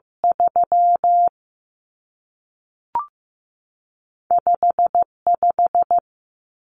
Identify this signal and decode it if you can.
Morse code, heard as a keyed tone, about 15 wpm, 700 Hz. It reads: 3 5 5
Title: Numbers - 2-Digit 15wpm